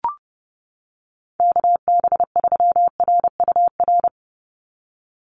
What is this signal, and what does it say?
Morse code, heard as a keyed tone, about 30 wpm, 700 Hz. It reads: KB3RUR
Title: USA Call Signs 30wpm